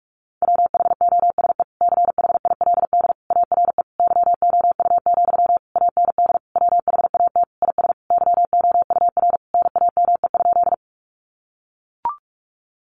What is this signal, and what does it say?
Morse code, heard as a keyed tone, about 44 wpm, 700 Hz. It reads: WHOSE CHILD ARE YOU, AND WHAT IS YOUR NAME?
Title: Sentences from Top 1200 Words 45wpm